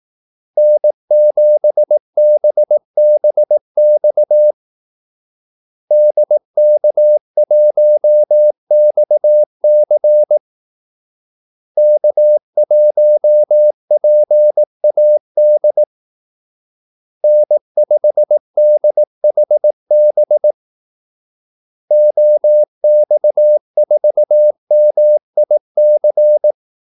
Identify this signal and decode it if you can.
Morse code, heard as a keyed tone, about 18 wpm, 600 Hz. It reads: N7BBX DK1XC K1PAD N5DHB OX4MIC
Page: Kaldesignaler 18 wpm